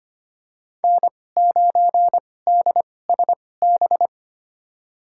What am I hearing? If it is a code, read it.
Morse code, heard as a keyed tone, about 25 wpm, spaced as for 18 wpm, 700 Hz. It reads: N9DSB
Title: USA Call Signs - 5 Characters - 2x Character Spacing - 25wpm